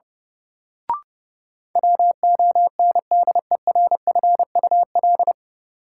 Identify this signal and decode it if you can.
Morse code, heard as a keyed tone, about 30 wpm, 700 Hz. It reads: WONDERFUL
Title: Sentences from Top 1100 Words - New Word Drill Alt - 30wpm